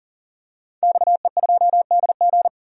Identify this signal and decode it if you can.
Morse code, heard as a keyed tone, about 40 wpm, 700 Hz. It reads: XE2DG
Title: DXCC-100 Call Signs - 5 Characters - 40wpm